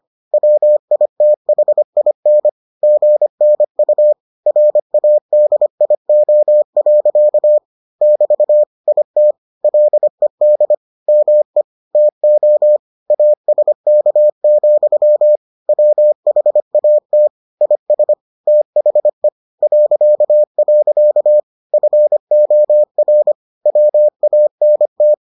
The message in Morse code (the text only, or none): WITHIN GNU RADIO. <BT> IT LED ME TO ASK, WHAT IS THE .. FOR WANT